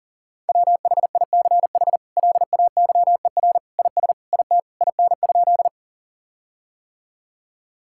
Morse code, heard as a keyed tone, about 37 wpm, 700 Hz. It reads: WHICH LAYER IS IT IN?
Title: Sentences from Top 1600 Words 40wpm